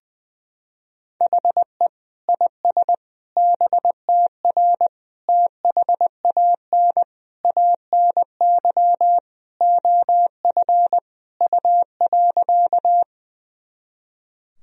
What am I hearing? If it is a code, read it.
Morse code, heard as a keyed tone, about 20 wpm, 700 Hz. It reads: HE IS BTR THAN ANY OF U.
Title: Sentences from Top 400 Words and Abbreviations 20wpm